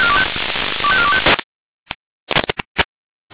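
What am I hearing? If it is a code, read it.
Five-tone selective call (ZVEI), heard as quick runs of five tones, tones of about 70 ms. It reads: 25425
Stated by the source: Five-tone ZVEI selective call 25414 received on SDR.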